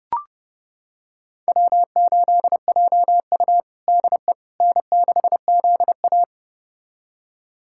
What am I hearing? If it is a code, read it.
Morse code, heard as a keyed tone, about 30 wpm, 700 Hz. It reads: W8JU DE N6ZA
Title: USA Callsign DE Callsign - 4 Characters - 30wpm